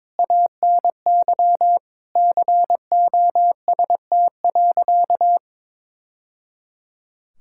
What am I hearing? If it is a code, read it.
Morse code, heard as a keyed tone, about 22 wpm, 700 Hz. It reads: ANY COST.
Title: Sentences from Top 1000 Words - Review Part 5 - 22wpm